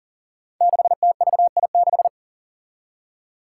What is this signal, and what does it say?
Morse code, heard as a keyed tone, about 40 wpm, 700 Hz. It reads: 6TVI6
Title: Five Challenging Letter-Number Combinations 40wpm